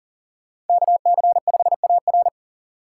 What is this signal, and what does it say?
Morse code, heard as a keyed tone, about 42 wpm, 700 Hz. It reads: KC5AR
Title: Speed Racing - USA Call Signs 40wpm